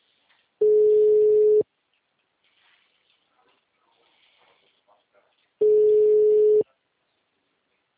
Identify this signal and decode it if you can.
Ringback tone (Europe, 425 Hz), with speech.